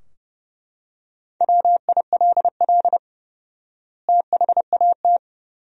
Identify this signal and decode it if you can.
Morse code, heard as a keyed tone, about 30 wpm, 700 Hz. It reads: WILL THAT
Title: 4x Word-Spacing - Sentences from Top 100 Words - 30wpm